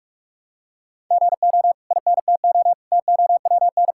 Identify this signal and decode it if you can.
Morse code, heard as a keyed tone, about 46 wpm, 700 Hz. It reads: GO INTO TOWN
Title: Sentences from Top 1000 Words - Review Part 6 - 45wpm